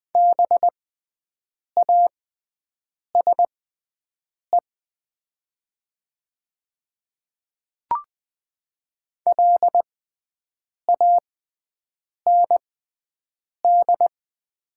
Morse code, heard as a keyed tone, about 20 wpm, spaced as for 7 wpm, 700 Hz. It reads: BASE LAND
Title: 6x Character Spacing - Top 300 Words - 20wpm